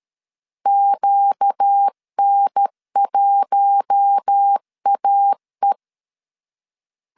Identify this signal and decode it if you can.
Morse code, heard as a keyed tone, about 13 wpm, 800 Hz. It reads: QN1AE
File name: CW / DET / 1 6 1500C S B 12wpm 3431